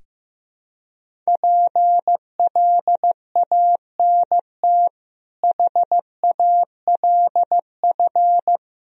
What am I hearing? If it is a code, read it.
Morse code, heard as a keyed tone, about 16 wpm, 700 Hz. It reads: PLANT HALF
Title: Sentences from Top 1000 Words - Review Part 1 - 15wpm